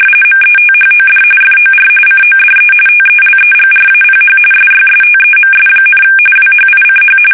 RTTY, about 45.45 baud. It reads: METHOD FOR CODE TELEGRAPH SYSTEMS, WHICH
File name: fsk850 / teletype-corp